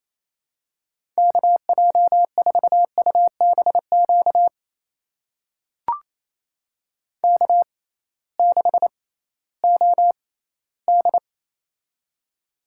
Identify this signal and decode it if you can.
Morse code, heard as a keyed tone, about 28 wpm, spaced as for 16 wpm, 700 Hz. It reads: KJ4UBQ K6OD
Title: USA Call Signs - 6x Character Spacing - 28wpm